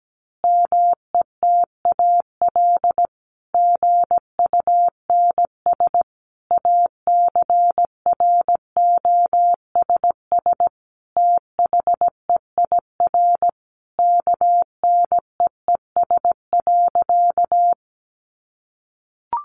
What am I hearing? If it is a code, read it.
Morse code, heard as a keyed tone, about 17 wpm, 700 Hz. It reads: METAL GUNS ACROSS THEIR KNEES.
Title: A Sound Of Thunder 17wpm